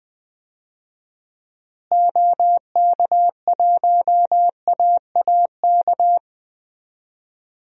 Morse code, heard as a keyed tone, about 20 wpm, 700 Hz. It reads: OK1AAK
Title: DXCC-100 Call Signs - 6 Characters - Repeated 2x - 20wpm